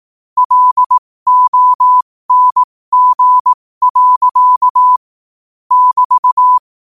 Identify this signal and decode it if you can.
Morse code, heard as a keyed tone, about 18 wpm, spaced as for 15 wpm, 1000 Hz. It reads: LONG. <BT>
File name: qotdfiles / 15wpm